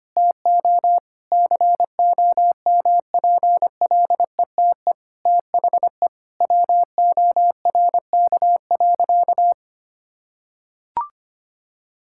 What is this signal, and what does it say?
Morse code, heard as a keyed tone, about 25 wpm, 700 Hz. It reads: TO COMPLETE THE WORK.
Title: Sentences from Top 1000 Words - Review Part 6 - 25wpm